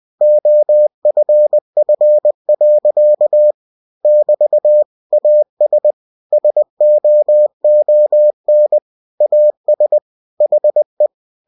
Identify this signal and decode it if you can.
Morse code, heard as a keyed tone, about 20 wpm, 600 Hz. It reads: OFF. <BT> AS SOON AS HE
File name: cw-books / winnie-the-pooh / winnie-20wpm-page-0037